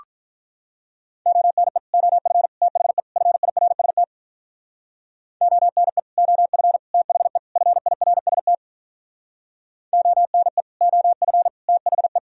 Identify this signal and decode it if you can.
Morse code, heard as a keyed tone, about 46 wpm, 700 Hz. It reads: ONE OF THE FIRST ONE OF THE FIRST ONE OF THE